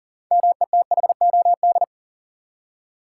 Morse code, heard as a keyed tone, about 40 wpm, 700 Hz. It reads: METHOD
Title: Six Letter Words - 40wpm